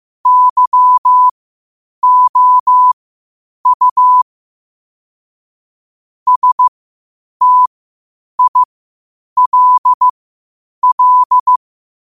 Morse code, heard as a keyed tone, about 15 wpm, spaced as for 8 wpm, 1000 Hz. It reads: YOU STILL
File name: qotdfiles / 8wpm